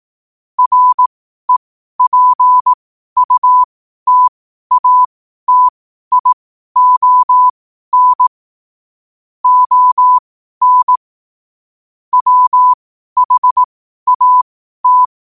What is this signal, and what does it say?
Morse code, heard as a keyed tone, about 18 wpm, spaced as for 12 wpm, 1000 Hz. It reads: REPUTATION ON WHAT
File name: qotdfiles / 12wpm